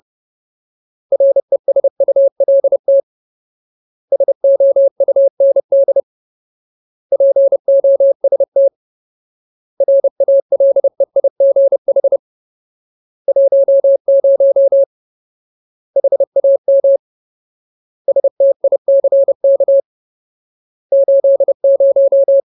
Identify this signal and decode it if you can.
Morse code, heard as a keyed tone, about 30 wpm, 550 Hz. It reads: RESULT SOUND POST RALEIGH 10 HAMSTICK 80
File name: Ultimate Word List / 30WPM Ultimate